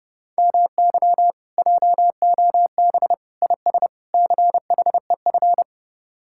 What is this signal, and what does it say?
Morse code, heard as a keyed tone, about 33 wpm, 700 Hz. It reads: MY JOB IS CHEF